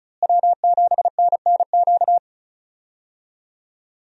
Morse code, heard as a keyed tone, about 34 wpm, 700 Hz. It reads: W7NNQ
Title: USA Call Signs - 5 Characters - Repeated 2x - 35wpm